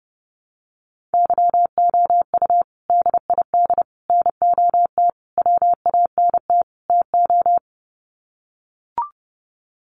Morse code, heard as a keyed tone, about 28 wpm, 700 Hz. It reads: YOU DID NOT WANT TO
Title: Sets of 5 Words Encore2 30wpm